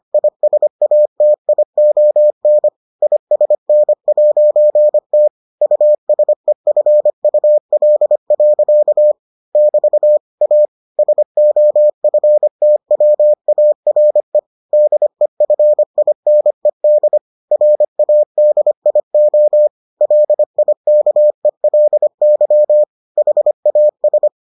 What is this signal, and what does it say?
Morse code, heard as a keyed tone, about 25 wpm, 600 Hz. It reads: ISATION ISN'T USEFUL. <BT> A SOFTWARE DEFINED RADIO LIKELY HAS